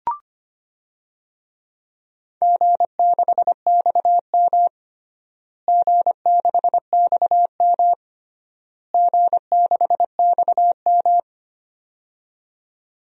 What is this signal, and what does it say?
Morse code, heard as a keyed tone, about 25 wpm, 700 Hz. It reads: G6XM G6XM G6XM
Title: DXCC-100 Call Signs - 4 Characters - Repeated 3x - 25wpm